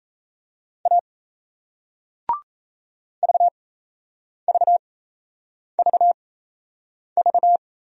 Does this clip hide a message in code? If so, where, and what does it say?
Yes: Morse code from 0.9 to 7.6 s, heard as a keyed tone, about 38 wpm, spaced as for 8 wpm, 700 Hz. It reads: A VVVV